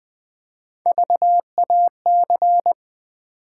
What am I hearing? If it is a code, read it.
Morse code, heard as a keyed tone, about 20 wpm, 700 Hz. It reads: VAC
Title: Build A Word - 3 Letters At A Time - 8-Letter Words - 20wpm